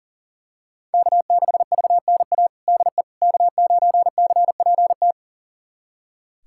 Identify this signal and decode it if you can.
Morse code, heard as a keyed tone, about 40 wpm, 700 Hz. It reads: K6VNA DE K9CPT
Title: USA Callsign DE Callsign - 5 Characters - 40wpm